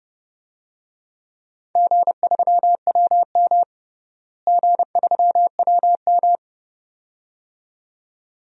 Morse code, heard as a keyed tone, about 32 wpm, 700 Hz. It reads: G3WM G3WM
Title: DXCC-100 Call Signs - 4 Characters - Repeated 2x - 30wpm